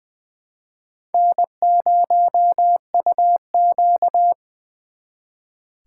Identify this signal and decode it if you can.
Morse code, heard as a keyed tone, about 25 wpm, 690 Hz. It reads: N0UQ
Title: Speed Racing - USA 1x2 Call Signs 20wpm